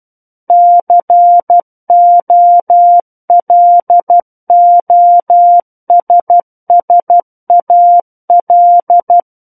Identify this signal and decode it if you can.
Morse code, heard as a keyed tone, about 12 wpm, 700 Hz. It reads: COLOSSAL
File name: Book One / 03-Chapter 3 - 12 WPM